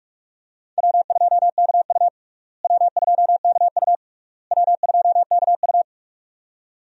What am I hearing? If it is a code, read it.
Morse code, heard as a keyed tone, about 41 wpm, 700 Hz. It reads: W2KU W2KU W2KU
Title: USA Call Signs - 4 Characters - Repeated 3x - 45wpm